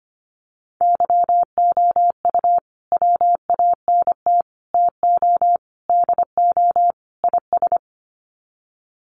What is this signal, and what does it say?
Morse code, heard as a keyed tone, about 25 wpm, 700 Hz. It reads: YOU WANT TO DO IS
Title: Sets of 5 Words Encore2 25wpm